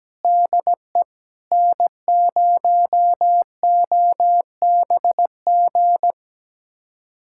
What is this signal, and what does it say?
Morse code, heard as a keyed tone, about 17 wpm, 700 Hz. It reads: DE N0OBG
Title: USA Callsign DE Callsign 17wpm